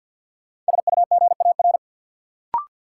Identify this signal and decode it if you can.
Morse code, heard as a keyed tone, about 51 wpm, 700 Hz. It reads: SUGAR
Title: Sentences from Top 1200 Words - New Word Drill - 50wpm